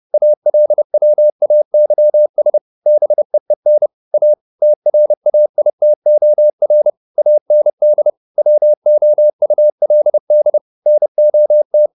Morse code, heard as a keyed tone, about 30 wpm, 600 Hz. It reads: ALWAYS BEEN A TRAITOR AND WOULD NOT